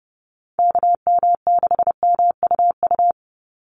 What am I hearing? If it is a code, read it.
Morse code, heard as a keyed tone, about 30 wpm, 700 Hz. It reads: KM6MUU
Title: USA Call Signs 30wpm